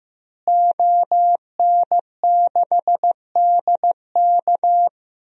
Morse code, heard as a keyed tone, about 16 wpm, 700 Hz. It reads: ON6DK
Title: DXCC-100 Call Signs - 5 Characters - Repeated 3x - 15wpm